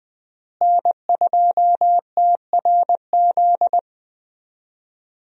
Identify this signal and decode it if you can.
Morse code, heard as a keyed tone, about 20 wpm, 700 Hz. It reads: N2TRZ
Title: USA Call Signs - 5 Characters - Repeated 2x - 20wpm